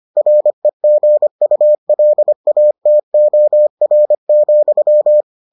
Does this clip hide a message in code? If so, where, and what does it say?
Yes: Morse code from 0.2 to 5.2 s, heard as a keyed tone, about 25 wpm, 600 Hz. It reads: REGULATOR,